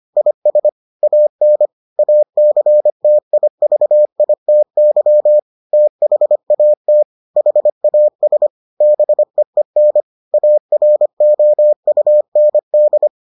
Morse code, heard as a keyed tone, about 25 wpm, 600 Hz. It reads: IS AN ACTIVITY THAT HAS BEEN AROUND